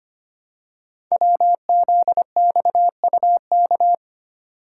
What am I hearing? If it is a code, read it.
Morse code, heard as a keyed tone, about 25 wpm, 700 Hz. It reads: WZXUK
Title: Code Groups - 5 Letter:Numbers - 25wpm